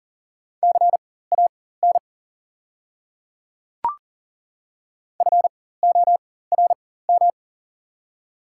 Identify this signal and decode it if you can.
Morse code, heard as a keyed tone, about 40 wpm, spaced as for 19 wpm, 700 Hz. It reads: CAN FORM
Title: 4x Character Spacing - Top 300 Words - 40wpm